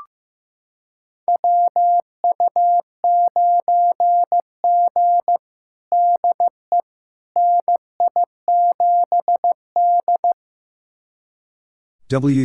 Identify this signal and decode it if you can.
Morse code, heard as a keyed tone, about 15 wpm, 700 Hz. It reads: WU9G DE NI7D
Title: USA Callsign DE Callsign - 4 Characters - 15wpm